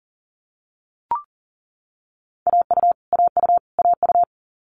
Morse code, heard as a keyed tone, about 40 wpm, 700 Hz. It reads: AU AU AU
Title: Multi-letter Phonemes Repeated 3x 40wpm